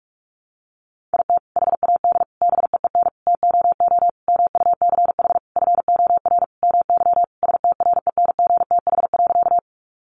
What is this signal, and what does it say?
Morse code, heard as a keyed tone, about 44 wpm, 700 Hz. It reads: IT HAD BEEN TOO MUCH FOR MY STRENGTH.
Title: Sentences from Top 1300 Words 45wpm